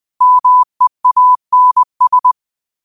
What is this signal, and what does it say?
Morse code, heard as a keyed tone, about 20 wpm, 1000 Hz. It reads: MEANS
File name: qotdfiles / 20wpm